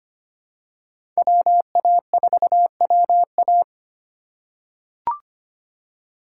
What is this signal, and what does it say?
Morse code, heard as a keyed tone, about 25 wpm, 700 Hz. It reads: WA4WA
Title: USA Call Signs - 5 Characters - 2x Character Spacing - 25wpm